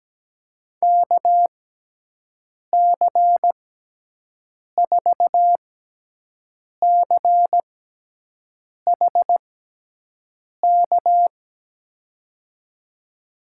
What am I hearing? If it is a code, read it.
Morse code, heard as a keyed tone, about 17 wpm, spaced as for 6 wpm, 700 Hz. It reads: KC4CHK
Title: USA Call Signs - 6x Character Spacing - 17wpm